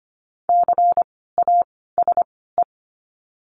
Morse code, heard as a keyed tone, about 25 wpm, spaced as for 16 wpm, 700 Hz. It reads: CASE
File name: course / 25:10wpm / 094 Lesson 16f - Letter c - Words 25wpm